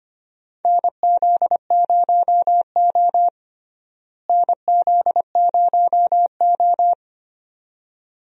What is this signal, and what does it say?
Morse code, heard as a keyed tone, about 25 wpm, 700 Hz. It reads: NZ0O NZ0O
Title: USA Call Signs - 4 Characters - Repeated 2x - 25wpm